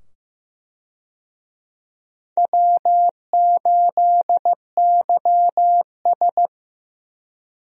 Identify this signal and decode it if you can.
Morse code, heard as a keyed tone, about 15 wpm, 700 Hz. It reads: W8YS
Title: USA Call Signs - 4 Characters - 15wpm